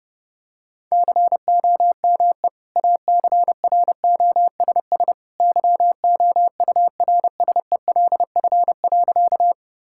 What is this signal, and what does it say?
Morse code, heard as a keyed tone, about 30 wpm, 700 Hz. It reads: COME ACROSS YOURSELF.
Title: Sentences from Top 1000 Words - Review Part 5 - 30wpm